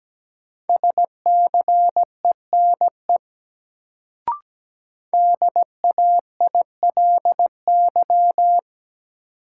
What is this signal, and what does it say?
Morse code, heard as a keyed tone, about 17 wpm, 700 Hz. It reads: SCENE DAILY
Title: Sentences from Top 1100 Words - New Word Drill Alt - 17wpm